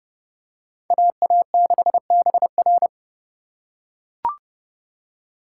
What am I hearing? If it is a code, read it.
Morse code, heard as a keyed tone, about 27 wpm, 700 Hz. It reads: AA6BR